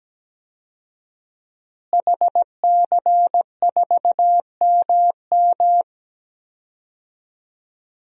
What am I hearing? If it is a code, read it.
Morse code, heard as a keyed tone, about 17 wpm, 700 Hz. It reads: HC4MM
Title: DXCC-100 Call Signs - 5 Characters - Repeated 2x - 17wpm